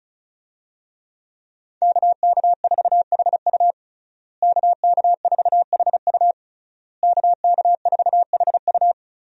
Morse code, heard as a keyed tone, about 36 wpm, 700 Hz. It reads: KK4HU KK4HU KK4HU
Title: USA Call Signs - 5 Characters - Repeated 3x - 35wpm